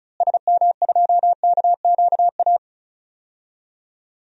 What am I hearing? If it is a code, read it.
Morse code, heard as a keyed tone, about 34 wpm, 700 Hz. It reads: SM2KQA
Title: DXCC-100 Call Signs - 6 Characters - Repeated 3x - 35wpm